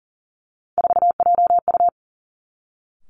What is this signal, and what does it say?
Morse code, heard as a keyed tone, about 38 wpm, 700 Hz. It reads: 4JU
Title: Three Challenging Letter-Number Combinations 40wpm